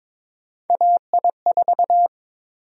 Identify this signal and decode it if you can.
Morse code, heard as a keyed tone, about 22 wpm, 700 Hz. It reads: AI4
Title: USA Call Sign Prefixes - 22wpm